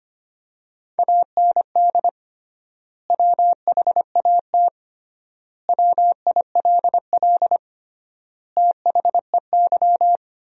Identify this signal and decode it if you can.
Morse code, heard as a keyed tone, about 25 wpm, 700 Hz. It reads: AND WHAT WILL THEY
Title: 3x Word-Spacing - Sentences from Top 200 Words - 25wpm